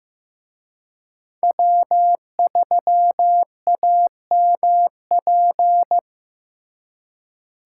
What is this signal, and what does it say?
Morse code, heard as a keyed tone, about 18 wpm, 700 Hz. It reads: W3AMP
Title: Speed Racing - USA 1x3 Call Signs 15wpm